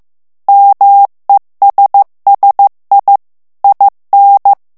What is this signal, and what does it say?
Morse code, heard as a keyed tone, about 15 wpm, 800 Hz. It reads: MESSI IN
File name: morse mp3 / ITALIEN texte clair 2 vitesse 14 mots minute